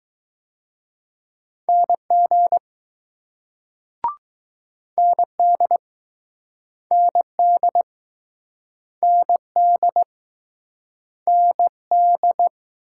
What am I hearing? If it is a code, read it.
Morse code, heard as a keyed tone, about 20 wpm, 700 Hz. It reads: NG ND ND ND ND